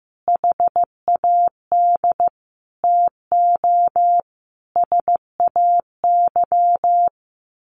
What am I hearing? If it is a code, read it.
Morse code, heard as a keyed tone, about 15 wpm, 700 Hz. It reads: HAD TO SAY
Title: Sets of 5 Words Encore2 15wpm